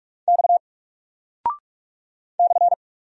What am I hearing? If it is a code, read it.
Morse code, heard as a keyed tone, about 44 wpm, 700 Hz. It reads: X /
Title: Single Character (Letter:Number:Punctuation:Prosign) 45wpm